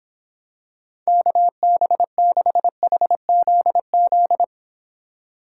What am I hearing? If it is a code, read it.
Morse code, heard as a keyed tone, about 26 wpm, 700 Hz. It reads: KB6HZZ